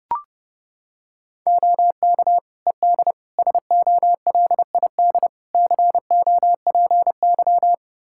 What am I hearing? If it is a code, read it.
Morse code, heard as a keyed tone, about 30 wpm, 700 Hz. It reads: OK ED SOLID COPY
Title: Practice QSO - Exchange 3 - More Info - 30wpm